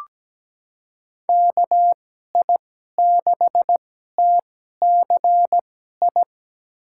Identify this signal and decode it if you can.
Morse code, heard as a keyed tone, about 17 wpm, spaced as for 12 wpm, 700 Hz. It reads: KI6TCI
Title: USA Call Signs - 2x Character Spacing - 17wpm